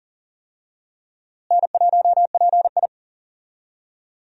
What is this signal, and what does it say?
Morse code, heard as a keyed tone, about 40 wpm, 700 Hz. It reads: N1PI
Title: USA Call Signs - 4 Characters - Repeated 2x - 40wpm